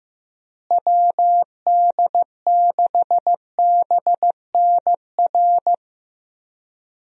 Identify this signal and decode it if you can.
Morse code, heard as a keyed tone, about 16 wpm, 700 Hz. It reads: WD6BNR
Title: USA Call Signs - 6 Characters - Repeated 2x - 15wpm